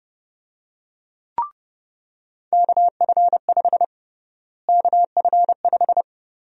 Morse code, heard as a keyed tone, about 32 wpm, 700 Hz. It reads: KF5 KF5
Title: USA Call Sign Prefixes - Repeated 2x - 30wpm